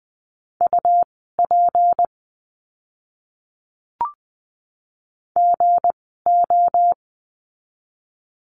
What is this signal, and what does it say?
Morse code, heard as a keyed tone, about 20 wpm, spaced as for 14 wpm, 700 Hz. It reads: UP GO
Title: 199 Lesson 32f - Letter k - Words 20wpm